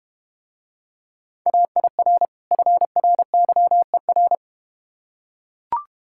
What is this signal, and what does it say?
Morse code, heard as a keyed tone, about 31 wpm, 700 Hz. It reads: AIR FRYER